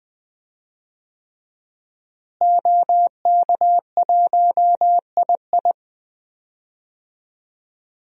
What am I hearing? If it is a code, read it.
Morse code, heard as a keyed tone, about 20 wpm, 700 Hz. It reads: OK1II
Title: DXCC-100 Call Signs - 5 Characters - Repeated 3x - 20wpm